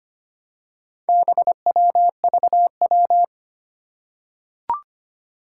Morse code, heard as a keyed tone, about 25 wpm, 700 Hz. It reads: BWVW
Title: Four Challenging Letter-Number Combinations 25wpm